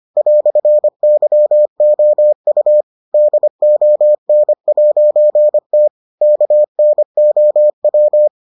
Morse code, heard as a keyed tone, about 25 wpm, 600 Hz. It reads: "YOU DON'T KNOW